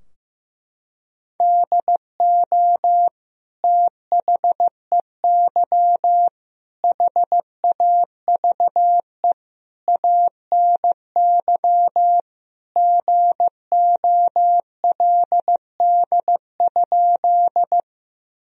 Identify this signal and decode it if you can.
Morse code, heard as a keyed tone, about 15 wpm, 700 Hz. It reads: DO THEY HAVE ANY GOLD?
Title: Sentences from Top 1000 Words - Review Part 3 - 15wpm